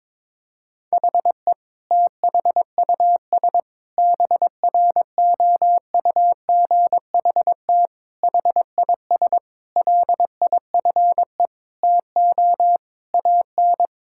Morse code, heard as a keyed tone, about 22 wpm, 700 Hz. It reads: HE THUS BROUGHT HIS LIFE TO AN
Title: Sentences from Top 1000 Words - Review Part 5 - 22wpm